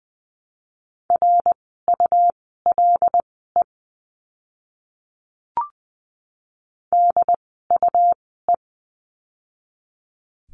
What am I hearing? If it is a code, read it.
Morse code, heard as a keyed tone, about 20 wpm, spaced as for 14 wpm, 700 Hz. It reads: RULE DUE
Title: 101 Lesson 17f - Letter u - Words 20wpm